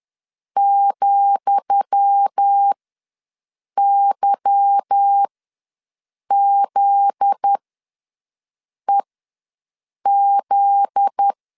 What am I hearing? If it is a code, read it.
Morse code, heard as a keyed tone, about 10 wpm, spaced as for 6 wpm, 800 Hz. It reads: ,YZEZ
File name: LEZIONI / 10 SIGNS / 10D COMA ws 10wpm 4314